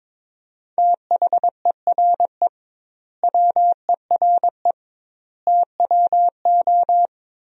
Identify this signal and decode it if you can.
Morse code, heard as a keyed tone, about 22 wpm, 700 Hz. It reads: THERE WERE TWO
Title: 2x Word-Spacing - Sentences from Top 100 Words - 22wpm